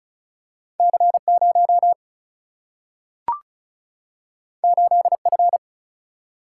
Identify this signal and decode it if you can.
Morse code, heard as a keyed tone, about 36 wpm, 700 Hz. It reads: C0 8F
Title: Single-Letter Single-Number 35wpm